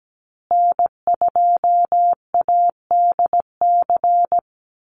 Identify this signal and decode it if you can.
Morse code, heard as a keyed tone, about 17 wpm, 700 Hz. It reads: N2ADC
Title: Call Signs Repeated 3x - 17wpm